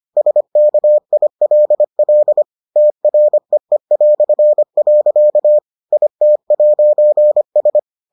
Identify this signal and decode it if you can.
Morse code, heard as a keyed tone, about 25 wpm, 600 Hz. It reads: SKILL TREE". IT'S